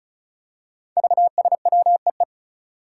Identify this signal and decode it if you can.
Morse code, heard as a keyed tone, about 36 wpm, 700 Hz. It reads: VSWEE
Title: Five Challenging Letter-Number Combinations 35wpm